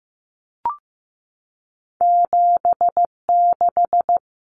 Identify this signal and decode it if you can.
Morse code, heard as a keyed tone, about 15 wpm, 700 Hz. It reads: 76
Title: Numbers - 2-Digit 15wpm